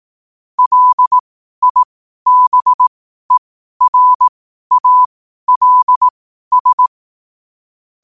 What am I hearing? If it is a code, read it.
Morse code, heard as a keyed tone, about 18 wpm, spaced as for 12 wpm, 1000 Hz. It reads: LIBERALS
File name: qotdfiles / 12wpm